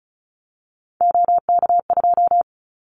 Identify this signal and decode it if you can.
Morse code, heard as a keyed tone, about 34 wpm, 700 Hz. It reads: OK2
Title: Callsign Prefixes for Top 100 DXCC - 35wpm